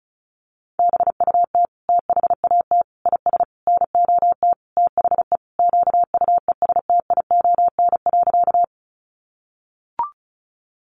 Morse code, heard as a keyed tone, about 34 wpm, 700 Hz. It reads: BUT THAT IS NOT THE QUESTION.
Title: Sentences from Top 600 Words 35wpm